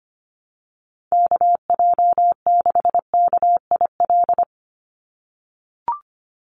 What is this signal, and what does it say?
Morse code, heard as a keyed tone, about 25 wpm, 700 Hz. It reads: KJ6KIL
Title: USA Call Signs 25wpm